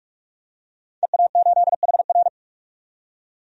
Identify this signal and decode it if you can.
Morse code, heard as a keyed tone, about 47 wpm, 700 Hz. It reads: EA8HR
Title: DXCC-100 Call Signs - 5 Characters - 45wpm